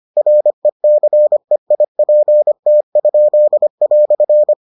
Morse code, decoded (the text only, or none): RECEIPT?"